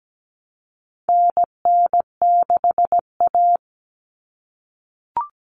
Morse code, heard as a keyed tone, about 17 wpm, 700 Hz. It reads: NN6A
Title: Common Cut Numbers - 4-Digit 17wpm